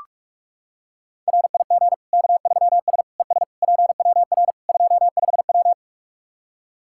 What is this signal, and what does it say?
Morse code, heard as a keyed tone, about 44 wpm, 700 Hz. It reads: RIG K3S ES PWR 25W
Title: Practice QSO - Exchange 2 - Info - 45wpm